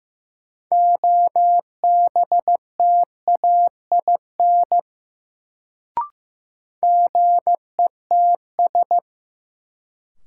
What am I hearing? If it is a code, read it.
Morse code, heard as a keyed tone, about 15 wpm, 700 Hz. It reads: OBTAIN GETS
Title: Sentences from Top 1300 Words - New Word Drill - 15wpm